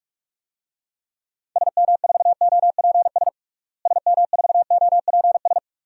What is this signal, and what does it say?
Morse code, heard as a keyed tone, about 46 wpm, 700 Hz. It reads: SM4OPS SM4OPS
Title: DXCC-100 Call Signs - 6 Characters - Repeated 3x - 45wpm